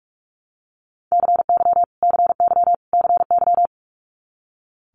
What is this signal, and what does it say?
Morse code, heard as a keyed tone, about 46 wpm, 700 Hz. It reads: CY CY CY
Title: Multi-letter Phonemes Repeated 3x 45wpm